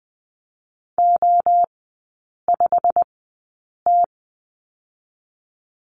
Morse code, heard as a keyed tone, about 20 wpm, spaced as for 8 wpm, 700 Hz. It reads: O5T
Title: 100 Lesson 17e - Letter u - Three Characters 20wpm